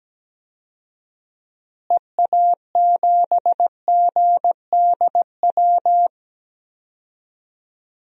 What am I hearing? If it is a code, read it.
Morse code, heard as a keyed tone, about 17 wpm, 700 Hz. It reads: EA7GDW
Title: DXCC-100 Call Signs - 6 Characters - Repeated 2x - 17wpm